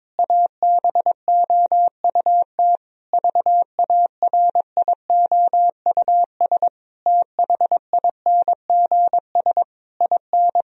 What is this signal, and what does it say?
Morse code, heard as a keyed tone, about 22 wpm, 700 Hz. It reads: ABOUT VARIOUS THINGS IN
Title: Sentences from Top 1000 Words - Review Part 4 - 22wpm